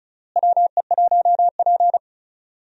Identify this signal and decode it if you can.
Morse code, heard as a keyed tone, about 36 wpm, 700 Hz. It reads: WE1P
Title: USA Call Signs - 4 Characters - Repeated 3x - 35wpm